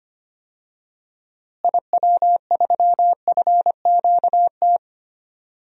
Morse code, heard as a keyed tone, about 25 wpm, 700 Hz. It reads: IW3FQT
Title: DXCC-100 Call Signs - 6 Characters - Repeated 3x - 25wpm